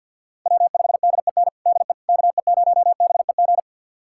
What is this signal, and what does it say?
Morse code, heard as a keyed tone, about 46 wpm, 700 Hz. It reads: W5DEN DE KE0BEG
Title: USA Callsign DE Callsign 50wpm